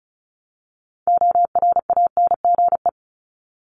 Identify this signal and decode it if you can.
Morse code, heard as a keyed tone, about 34 wpm, 700 Hz. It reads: ORANGE
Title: Top 1751-2000 Words 35wpm